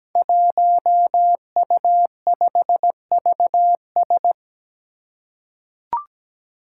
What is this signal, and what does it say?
Morse code, heard as a keyed tone, about 17 wpm, 700 Hz. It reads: 1U5VS
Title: Five Challenging Letter-Number Combinations 17wpm